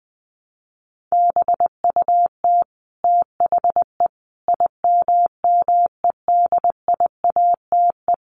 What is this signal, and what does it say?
Morse code, heard as a keyed tone, about 20 wpm, 700 Hz. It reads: BUT THE IMMEDIATE
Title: A Sound Of Thunder 20wpm